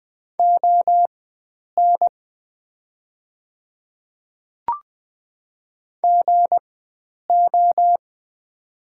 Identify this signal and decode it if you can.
Morse code, heard as a keyed tone, about 20 wpm, spaced as for 9 wpm, 700 Hz. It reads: ON GO
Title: 4x Character Spacing - Top 100 Words - 20wpm